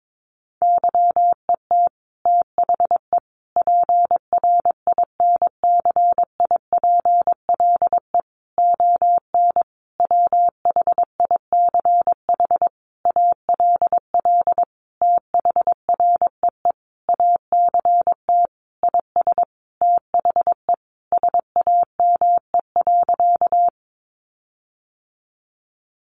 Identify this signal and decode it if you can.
Morse code, heard as a keyed tone, about 22 wpm, 700 Hz. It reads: YET THE PRINCIPLE ON WHICH ALL THREE ACT IS THE SAME.
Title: Sentences from Top 1400 Words 22wpm